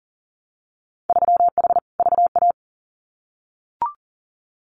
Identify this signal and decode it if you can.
Morse code, heard as a keyed tone, about 40 wpm, 700 Hz. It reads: 3H VA
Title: Winter Field Day Exchanges 40wpm